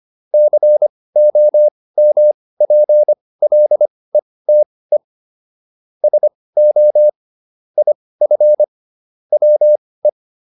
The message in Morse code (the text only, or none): COMPLETE SO IF WE